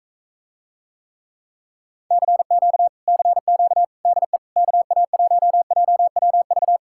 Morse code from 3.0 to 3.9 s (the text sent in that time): CQ